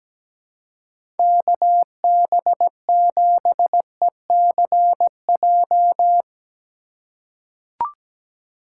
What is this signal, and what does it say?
Morse code, heard as a keyed tone, about 17 wpm, 700 Hz. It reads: KB7ECJ
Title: USA Call Signs - 6 Characters - 2x Character Spacing - 17wpm